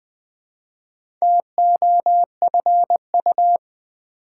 Morse code, heard as a keyed tone, about 20 wpm, 700 Hz. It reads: TOFU
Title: Difficult Words with Hints - 20wpm